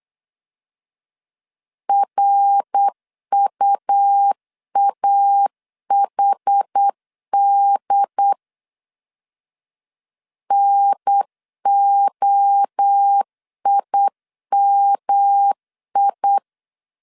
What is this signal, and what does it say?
Morse code, heard as a keyed tone, about 8 wpm, 800 Hz. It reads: RUAHD NOIMI
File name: CW / DET / 1 3 1500C S B 8wpm 4141